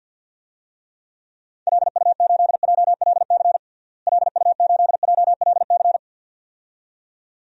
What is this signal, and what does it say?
Morse code, heard as a keyed tone, about 47 wpm, 700 Hz. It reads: LU8JLC LU8JLC
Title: DXCC-100 Call Signs - 6 Characters - Repeated 2x - 50wpm